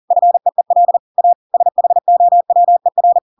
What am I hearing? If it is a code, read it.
Morse code, heard as a keyed tone, about 40 wpm, 700 Hz. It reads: FEEL A SHOWER
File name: Pinocchio / Pinocchio ch6 40 40 ik1hge